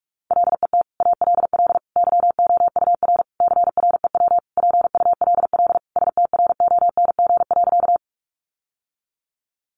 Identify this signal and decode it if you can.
Morse code, heard as a keyed tone, about 44 wpm, 700 Hz. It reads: LET ALL YOUR CREW PULL STRONG.
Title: Sentences from Top 1500 Words 45wpm